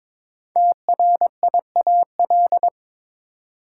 Morse code, heard as a keyed tone, about 22 wpm, 700 Hz. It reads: TRIAL
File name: 5 Letter Words - First 3 Letters Given - 22wpm